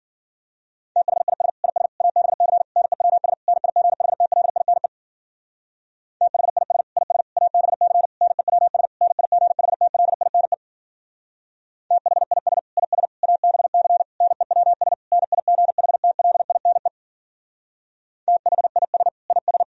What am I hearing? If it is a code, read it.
Morse code, heard as a keyed tone, about 51 wpm, 700 Hz. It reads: THIS IS ABC NEWS NIGHTLINE THIS IS ABC NEWS NIGHTLINE THIS IS ABC NEWS NIGHTLINE THIS IS